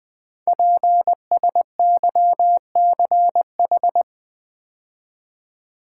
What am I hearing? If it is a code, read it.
Morse code, heard as a keyed tone, about 20 wpm, 700 Hz. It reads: PSYCH
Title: Build A Word - 5 Letters At A Time - 10-Letter Words - 20wpm